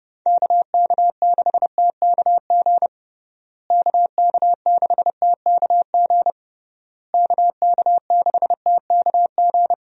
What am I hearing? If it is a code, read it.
Morse code, heard as a keyed tone, about 30 wpm, 700 Hz. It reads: KK6TKG KK6TKG KK6TKG
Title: USA Call Signs - 6 Characters - Repeated 3x - 30wpm